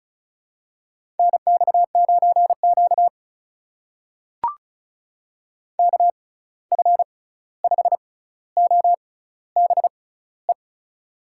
Morse code, heard as a keyed tone, about 36 wpm, spaced as for 16 wpm, 700 Hz. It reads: NX9Q KF5OBE
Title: USA Call Signs - 6x Character Spacing - 35wpm